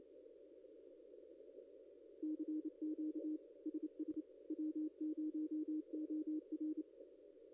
Morse code, heard as a keyed tone, about 27 wpm, 310 Hz. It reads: CQ SS W0OR
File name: contests / ss-cw-k5tr-op-n5rz -0900